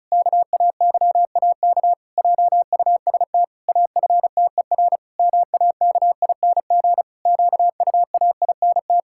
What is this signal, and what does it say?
Morse code, heard as a keyed tone, about 36 wpm, 700 Hz. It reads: KAYAK JUST AFTER MAKING QUAINT
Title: Pangram Challenge v2 - 35wpm